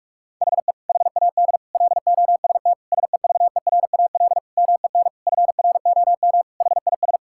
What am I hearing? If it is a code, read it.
Morse code, heard as a keyed tone, about 46 wpm, 700 Hz. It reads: HE HAD LOST SEVERAL MEN FROM HIS
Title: Sentences from Top 1600 Words 45wpm